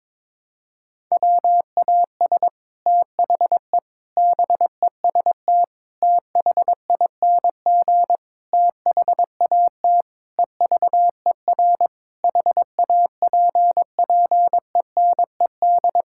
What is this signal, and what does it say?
Morse code, heard as a keyed tone, about 22 wpm, 700 Hz. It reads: WAS THE BEST THING THAT EVER HAPPENED
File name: practice / 22wpm / Sets of 8 Words - 22wpm